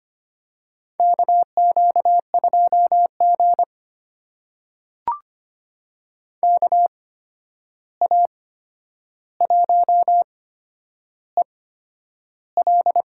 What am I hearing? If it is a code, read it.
Morse code, heard as a keyed tone, about 25 wpm, spaced as for 10 wpm, 700 Hz. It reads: KQ2G KA1EL
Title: USA Call Signs - 8x Character Spacing - 25wpm